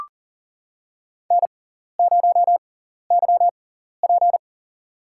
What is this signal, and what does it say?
Morse code, heard as a keyed tone, about 40 wpm, spaced as for 14 wpm, 700 Hz. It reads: N0YP
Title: USA Call Signs - 4 Characters - 6x Character Spacing - 40wpm